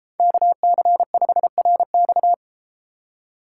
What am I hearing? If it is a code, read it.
Morse code, heard as a keyed tone, about 33 wpm, 700 Hz. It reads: KC5RX